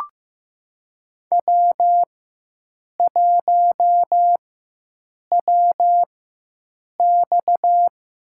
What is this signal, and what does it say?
Morse code, heard as a keyed tone, about 15 wpm, spaced as for 7 wpm, 700 Hz. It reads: W1WX
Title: USA Call Signs - 4 Characters - 4x Character Spacing - 15wpm